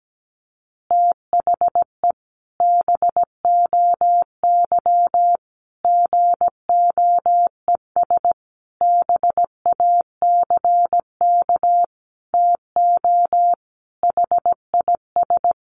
Morse code, heard as a keyed tone, about 17 wpm, 700 Hz. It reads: THE BOY GOES BACK TO HIS
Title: Easy Phrases 17wpm